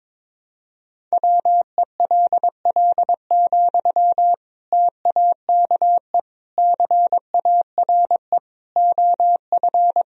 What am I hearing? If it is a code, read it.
Morse code, heard as a keyed tone, about 22 wpm, 700 Hz. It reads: WELL, TAKE CARE OF
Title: Sentences from Top 1000 Words - Review Part 1 - 22wpm